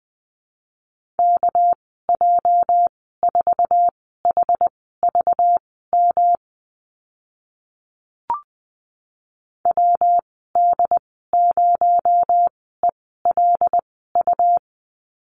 Morse code, heard as a keyed tone, about 20 wpm, spaced as for 14 wpm, 700 Hz. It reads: KJ4HVM WD0ELU
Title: 219 Lesson 35g - Number 0 - Call Signs 20wpm